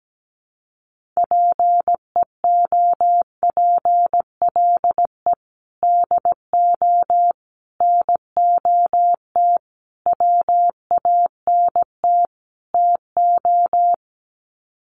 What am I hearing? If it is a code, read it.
Morse code, heard as a keyed tone, about 17 wpm, 700 Hz. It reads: PEOPLE DO NOT WANT TO
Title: Sets of 5 Words Encore2 17wpm